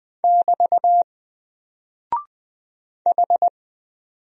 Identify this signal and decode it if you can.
Morse code, heard as a keyed tone, about 20 wpm, 700 Hz. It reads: <BT> H
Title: Single Character (Letter:Number:Punctuation:Prosign) Alt 20wpm